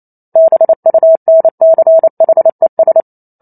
Morse code, heard as a keyed tone, about 28 wpm, 650 Hz. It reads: BUNCHES
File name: Chapter 2 - 28 WPM